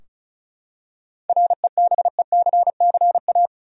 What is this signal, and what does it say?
Morse code, heard as a keyed tone, about 36 wpm, 700 Hz. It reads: REBECCA
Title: USA Common Names - 35wpm